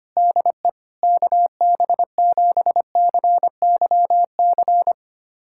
Morse code, heard as a keyed tone, about 25 wpm, 700 Hz. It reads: DE KB7CYC
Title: USA Callsign DE Callsign - 6 Characters - 25wpm